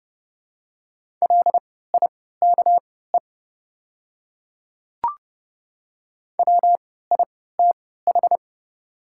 Morse code, heard as a keyed tone, about 30 wpm, spaced as for 17 wpm, 700 Hz. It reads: LIKE WITH
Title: 3x Character Spacing - Top 200 Words - 30wpm